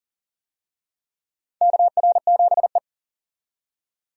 Morse code, heard as a keyed tone, about 40 wpm, 700 Hz. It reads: KR7E
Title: USA Call Signs - 4 Characters - 40wpm